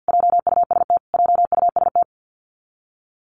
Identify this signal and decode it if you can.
Morse code, heard as a keyed tone, about 50 wpm, 700 Hz. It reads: JUST JUST
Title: Top 500 Words Repeated 50wpm